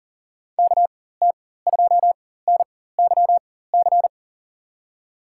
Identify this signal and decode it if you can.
Morse code, heard as a keyed tone, about 40 wpm, spaced as for 19 wpm, 700 Hz. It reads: KT2NYC
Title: USA Call Signs - 6 Characters - 4x Character Spacing - 40wpm